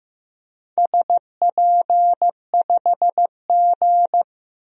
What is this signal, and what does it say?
Morse code, heard as a keyed tone, about 15 wpm, 700 Hz. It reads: SP5G
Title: DXCC-100 Call Signs - 4 Characters - Repeated 3x - 15wpm